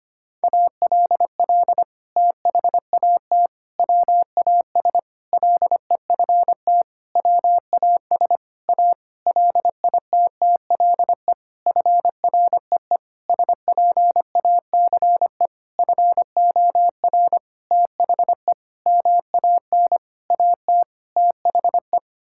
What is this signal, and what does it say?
Morse code, heard as a keyed tone, about 25 wpm, 700 Hz. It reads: ALL THAT WAS LEFT WAS A LITTLE FREE SPACE FOR THE MAN AT THE
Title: Sentences from Top 1000 Words - Review Part 5 - 25wpm